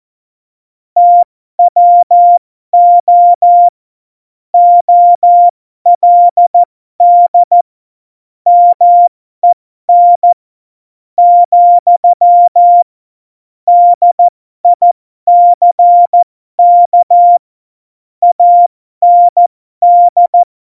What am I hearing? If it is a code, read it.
Morse code, heard as a keyed tone, about 14 wpm, spaced as for 12 wpm, 700 Hz. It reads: TWO OLD MEN , DICK AND
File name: Morsekurs / Witze